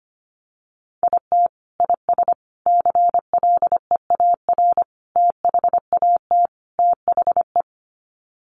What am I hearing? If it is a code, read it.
Morse code, heard as a keyed tone, about 25 wpm, 700 Hz. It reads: IT IS CLEAR THAT THE
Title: Sets of 5 Words Encore 25wpm